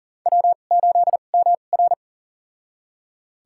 Morse code, heard as a keyed tone, about 37 wpm, spaced as for 28 wpm, 700 Hz. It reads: W8MR
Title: USA Call Signs - 4 Characters - 2x Character Spacing - 40wpm